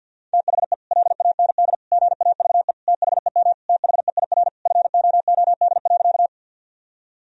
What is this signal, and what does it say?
Morse code, heard as a keyed tone, about 52 wpm, 700 Hz. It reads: THE LAND GAVE THEM THEIR FOOD.
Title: Sentences from Top 1000 Words - Review Part 1 - 50wpm